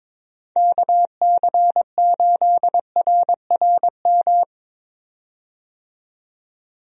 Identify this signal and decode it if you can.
Morse code, heard as a keyed tone, about 22 wpm, 700 Hz. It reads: KC8RRM
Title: USA Call Signs - 6 Characters - 22wpm